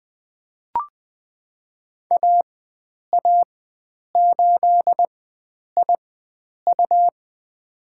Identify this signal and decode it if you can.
Morse code, heard as a keyed tone, about 21 wpm, spaced as for 9 wpm, 700 Hz. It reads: AA8IU
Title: USA Call Signs - 5 Characters - 4x Character Spacing - 20wpm